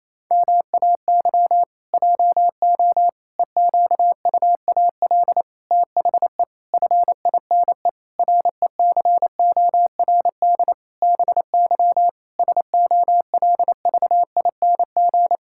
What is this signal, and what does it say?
Morse code, heard as a keyed tone, about 28 wpm, 700 Hz. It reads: MAY JO EQUAL THE FINE RECORD BY SOLVING
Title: Pangram Challenge v2 - 28wpm